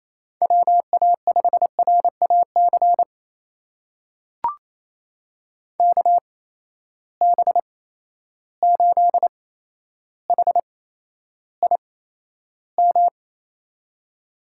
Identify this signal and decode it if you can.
Morse code, heard as a keyed tone, about 28 wpm, spaced as for 12 wpm, 700 Hz. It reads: WA5RAC KB8HIM
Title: USA Call Signs - 6 Characters - 8x Character Spacing - 28wpm